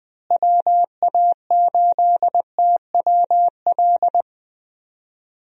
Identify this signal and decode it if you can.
Morse code, heard as a keyed tone, about 21 wpm, 700 Hz. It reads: WA8TWL
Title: USA Call Signs - 6 Characters - Repeated 3x - 20wpm